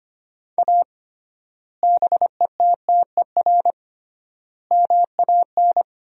Morse code, heard as a keyed tone, about 25 wpm, 700 Hz. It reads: A BETTER MAN
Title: 3x Word-Spacing - Sentences from Top 400 Words - 25wpm